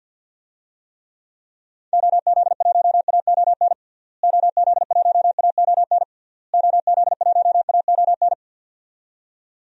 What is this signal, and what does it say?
Morse code, heard as a keyed tone, about 49 wpm, 700 Hz. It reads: OZ1AON OZ1AON OZ1AON
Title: DXCC-100 Call Signs - 6 Characters - Repeated 3x - 50wpm